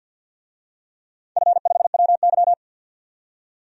Morse code, heard as a keyed tone, about 51 wpm, 700 Hz. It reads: F5WY
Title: DXCC-100 Call Signs - 4 Characters - 50wpm